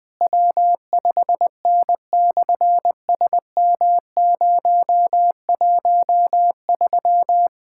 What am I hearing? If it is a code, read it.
Morse code, heard as a keyed tone, about 20 wpm, 700 Hz. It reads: W5N/SM013
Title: SOTA QSO Practice 20wpm